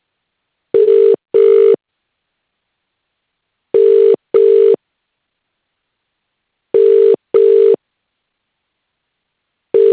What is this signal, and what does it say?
Ringback tone (UK), 400 + 450 Hz.